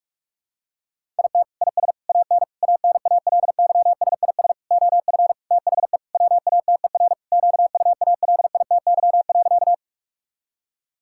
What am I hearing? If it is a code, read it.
Morse code, heard as a keyed tone, about 46 wpm, 700 Hz. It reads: IT IS AN ANALYSIS OF THE WATER QUALITY.
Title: Sentences from Top 1000 Words - Review Part 5 - 45wpm